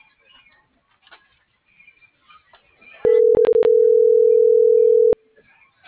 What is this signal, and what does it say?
Ringback tone (North America), 440 + 480 Hz.